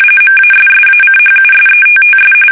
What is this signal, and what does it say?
RTTY, about 45.45 baud. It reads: MASSACHUSETTS
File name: fsk850 / USS-Massachusetts